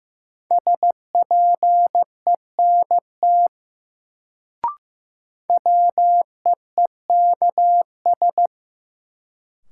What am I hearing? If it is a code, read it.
Morse code, heard as a keyed tone, about 15 wpm, 700 Hz. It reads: SPENT WEEKS
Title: Sets of 5 Words - New Word Drill - 15wpm